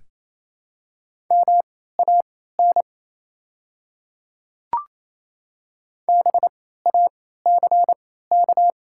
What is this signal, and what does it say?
Morse code, heard as a keyed tone, about 28 wpm, spaced as for 16 wpm, 700 Hz. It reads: MAN BACK
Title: 3x Character Spacing - Top 200 Words - 28wpm